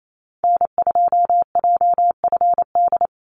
Morse code, heard as a keyed tone, about 28 wpm, 690 Hz. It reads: N2JFD
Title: Call Signs Repeated 3x - 28wpm